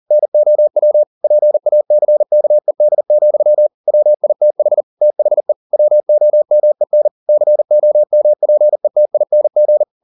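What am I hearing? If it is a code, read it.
Morse code, heard as a keyed tone, about 40 wpm, 600 Hz. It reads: NOW PACKED, WITH THE WOMEN COMPETING
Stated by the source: Morse code at 40 wpm, practice text, CW med Gnister.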